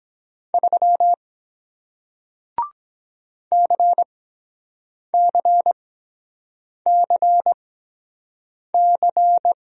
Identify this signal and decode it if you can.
Morse code, heard as a keyed tone, about 23 wpm, 700 Hz. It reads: 3 C C C C